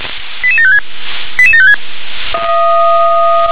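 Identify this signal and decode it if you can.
Five-tone selective call (ZVEI), heard as quick runs of five tones, tones of about 70 ms. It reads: 88756 88756
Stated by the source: ZVEI five-tone selective call received on SDR.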